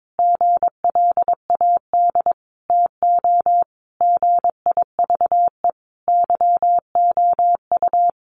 Morse code, heard as a keyed tone, about 22 wpm, 700 Hz. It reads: GLAD TO GIVE YOU
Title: Sentences from Top 800 Words 22wpm